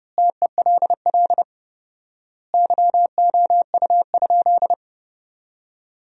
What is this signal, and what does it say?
Morse code, heard as a keyed tone, about 30 wpm, 700 Hz. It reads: TELL YOU?
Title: 4x Word-Spacing - Sentences from Top 200 Words - 30wpm